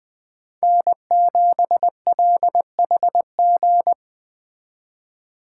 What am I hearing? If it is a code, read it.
Morse code, heard as a keyed tone, about 19 wpm, 700 Hz. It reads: N7LHG